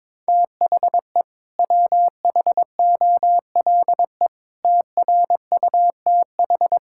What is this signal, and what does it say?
Morse code, heard as a keyed tone, about 21 wpm, 700 Hz. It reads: THE WHOLE TRUTH
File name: practice / 22wpm / Sets of 8 Words - 22wpm